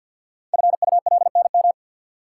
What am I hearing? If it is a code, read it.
Morse code, heard as a keyed tone, about 50 wpm, 690 Hz. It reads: FULNM
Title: Code Groups - 5 Letters - 50wpm